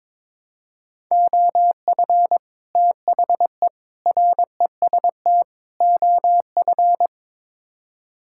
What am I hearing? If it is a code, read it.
Morse code, heard as a keyed tone, about 21 wpm, 700 Hz. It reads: OF THE REST OF
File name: practice / 22wpm / Build A Phrase - 5-Word Phrases - 22wpm